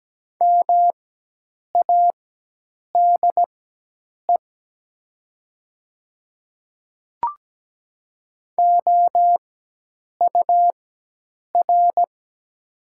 Morse code, heard as a keyed tone, about 17 wpm, spaced as for 8 wpm, 700 Hz. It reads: MADE OUR
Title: 4x Character Spacing - Top 500 Words - 17wpm